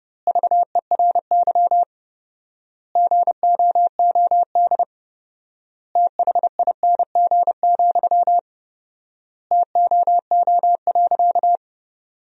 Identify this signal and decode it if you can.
Morse code, heard as a keyed tone, about 31 wpm, 700 Hz. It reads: VERY GOOD THING, TOO.
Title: 4x Word-Spacing - Sentences from Top 200 Words - 30wpm